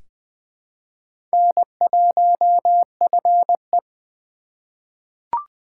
Morse code, heard as a keyed tone, about 21 wpm, 700 Hz. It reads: N1FE
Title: K1USN SST QSO Practice - 20wpm